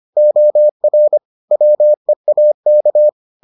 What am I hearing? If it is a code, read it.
Morse code, heard as a keyed tone, about 25 wpm, 600 Hz. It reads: OR WEAK